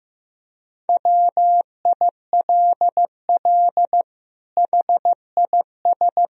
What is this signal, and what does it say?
Morse code, heard as a keyed tone, about 15 wpm, 700 Hz. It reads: WILL HIS
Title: Sentences from Top 1000 Words - Review Part 4 - 15wpm